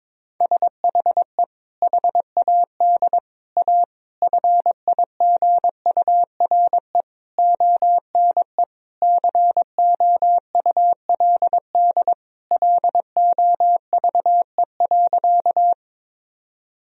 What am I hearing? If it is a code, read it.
Morse code, heard as a keyed tone, about 22 wpm, 700 Hz. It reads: SHE HAD A FIGURE ONE COULD LOVE.
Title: Sentences from Top 1000 Words - Review Part 2 - 22wpm